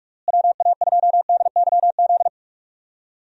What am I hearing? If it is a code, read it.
Morse code, heard as a keyed tone, about 46 wpm, 700 Hz. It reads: WA2DYZ
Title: USA Call Signs - 6 Characters - Repeated 3x - 45wpm